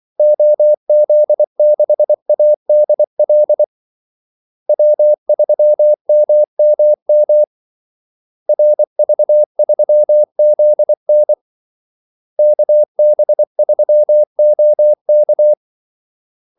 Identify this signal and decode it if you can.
Morse code, heard as a keyed tone, about 24 wpm, 600 Hz. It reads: OZ6ADL W3MMM RV3ZN KB3OK
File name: Callsigns-24wpm